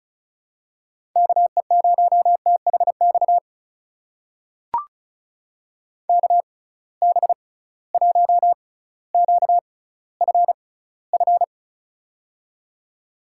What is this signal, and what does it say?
Morse code, heard as a keyed tone, about 36 wpm, spaced as for 18 wpm, 700 Hz. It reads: KE0THX KB1QFF
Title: USA Call Signs - 6 Characters - 6x Character Spacing - 35wpm